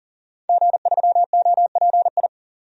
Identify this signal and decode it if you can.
Morse code, heard as a keyed tone, about 40 wpm, 700 Hz. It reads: G3OPI
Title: DXCC-100 Call Signs - 5 Characters - 40wpm